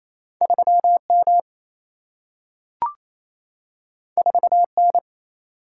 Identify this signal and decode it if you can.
Morse code, heard as a keyed tone, about 28 wpm, 700 Hz. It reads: 3M 4N
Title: Single-Letter Single-Number 28wpm